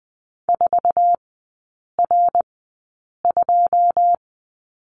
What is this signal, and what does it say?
Morse code, heard as a keyed tone, about 20 wpm, spaced as for 8 wpm, 700 Hz. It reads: 4R2
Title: 159 Lesson 26e - Letter g - Three Characters 20wpm